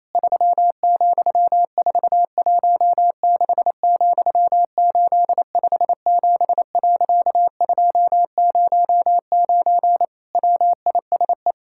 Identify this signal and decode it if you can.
Morse code, heard as a keyed tone, about 28 wpm, 700 Hz. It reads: 3,416,857.209 WISE
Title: Pangram Challenge v2 - 28wpm